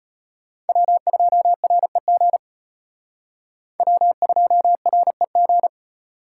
Morse code, heard as a keyed tone, about 32 wpm, 700 Hz. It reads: W2REG W2REG